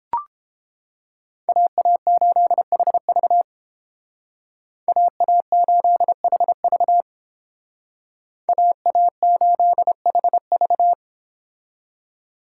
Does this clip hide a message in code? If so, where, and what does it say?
Yes: Morse code from 1.5 to 10.9 s, heard as a keyed tone, about 31 wpm, 700 Hz. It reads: AA8HV AA8HV AA8HV